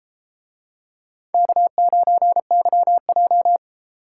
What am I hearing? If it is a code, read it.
Morse code, heard as a keyed tone, about 32 wpm, 700 Hz. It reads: K9YJ